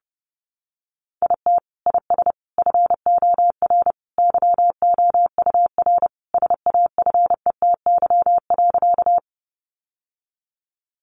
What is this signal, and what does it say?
Morse code, heard as a keyed tone, about 30 wpm, 700 Hz. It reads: IT IS FOR YOUR SAFETY.
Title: Sentences from Top 700 Words 30wpm